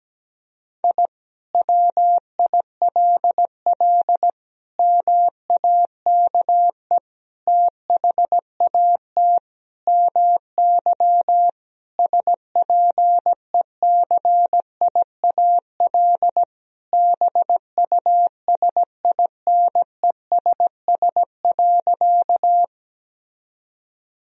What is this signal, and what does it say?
Morse code, heard as a keyed tone, about 17 wpm, 700 Hz. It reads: I WILL MAKE THAT MY SPECIAL BUSINESS.
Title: Sentences from Top 1000 Words - Review Part 3 - 17wpm